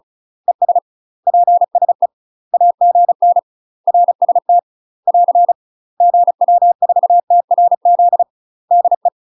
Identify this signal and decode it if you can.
Morse code, heard as a keyed tone, about 35 wpm, 700 Hz. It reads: ES PSE AGN RST <AR> GW4TRZ DE
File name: UK MASTER QSOs TRAINING / 35WPM / UK MASTER QSOs 0029